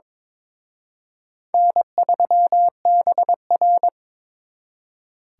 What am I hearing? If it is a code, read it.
Morse code, heard as a keyed tone, about 26 wpm, 700 Hz. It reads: N3BR
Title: Speed Racing - USA 1x2 Call Signs 22wpm